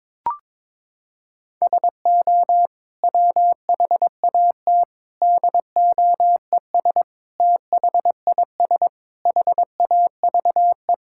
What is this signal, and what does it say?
Morse code, heard as a keyed tone, about 21 wpm, 700 Hz. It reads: SO WHAT DOES THIS HAVE
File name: practice / 22wpm / Sets of 8 Words - 22wpm